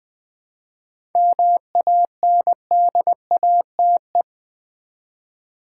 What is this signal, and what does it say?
Morse code, heard as a keyed tone, about 21 wpm, 700 Hz. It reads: MANDATE
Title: 7 Letter Words - First 3 Letters Given - 20wpm